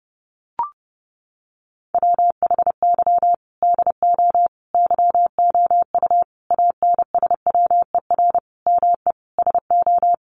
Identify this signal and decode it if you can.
Morse code, heard as a keyed tone, about 30 wpm, 700 Hz. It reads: WHY DO YOU ANSWER ME SO
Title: Sentences from Top 1500 Words 30wpm